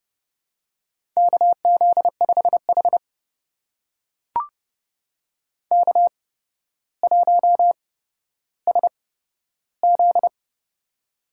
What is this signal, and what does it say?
Morse code, heard as a keyed tone, about 30 wpm, spaced as for 13 wpm, 700 Hz. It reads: KZ5H K1SZ
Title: USA Call Signs - 4 Characters - 8x Character Spacing - 30wpm